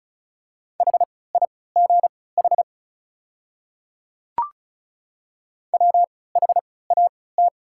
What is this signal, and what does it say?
Morse code, heard as a keyed tone, about 36 wpm, spaced as for 20 wpm, 700 Hz. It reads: HIGH WHAT
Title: 3x Character Spacing - Top 200 Words - 35wpm